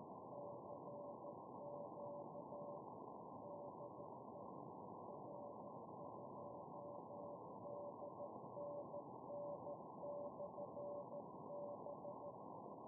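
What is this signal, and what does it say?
Morse code, heard as a keyed tone, about 13 wpm, 580 Hz, with noise. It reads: CQ DE R2DNN/B
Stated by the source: 120 mW QRP beacon UI7K, heard with QSB fading to zero in Kirov oblast and recorded on an SDR; it sends temperature and supply-voltage digits.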